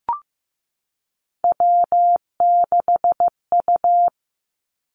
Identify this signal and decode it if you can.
Morse code, heard as a keyed tone, about 15 wpm, 700 Hz. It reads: W6U
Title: Three Challenging Letter-Number Combinations 15wpm